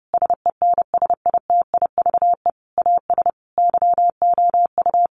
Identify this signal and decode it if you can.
Morse code, heard as a keyed tone, about 30 wpm, 700 Hz. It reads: SENSITIVE AS YOU
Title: Sentences from Top 1400 Words 30wpm